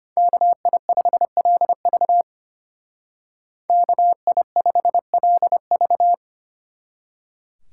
Morse code, heard as a keyed tone, about 31 wpm, 700 Hz. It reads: KI5LV KI5LV
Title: Speed Racing - USA 2x2 Call Signs 25wpm